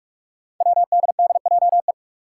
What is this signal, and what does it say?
Morse code, heard as a keyed tone, about 44 wpm, 700 Hz. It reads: WDDJE
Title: Five Challenging Letter-Number Combinations 45wpm